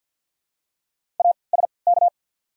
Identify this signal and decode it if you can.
Morse code, heard as a keyed tone, about 53 wpm, spaced as for 29 wpm, 700 Hz. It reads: ASK
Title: 3x Character Spacing - Top 500 Words - 50wpm